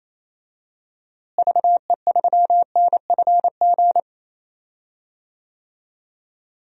Morse code, heard as a keyed tone, about 28 wpm, 700 Hz. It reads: VE3NFG
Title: DXCC-100 Call Signs - 6 Characters - Repeated 3x - 28wpm